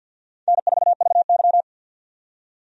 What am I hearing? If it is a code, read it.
Morse code, heard as a keyed tone, about 49 wpm, 700 Hz. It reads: N4VY
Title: USA Call Signs - 4 Characters - 50wpm